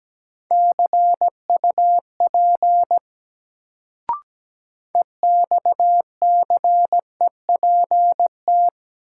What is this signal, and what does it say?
Morse code, heard as a keyed tone, about 17 wpm, 700 Hz. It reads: CUP EXCEPT
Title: Sentences from Top 1600 Words - New Word Drill - 17wpm